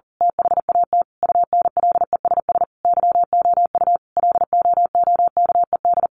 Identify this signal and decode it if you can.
Morse code, heard as a keyed tone, about 40 wpm, 700 Hz. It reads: THAT UNLESS YOU LOOKED
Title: A Sound Of Thunder 40wpm